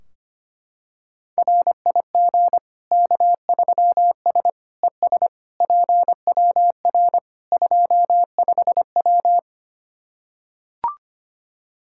Morse code, heard as a keyed tone, about 25 wpm, 700 Hz. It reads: RIG K3S ES PWR 25W
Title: Practice QSO - Exchange 2 - Info - 25wpm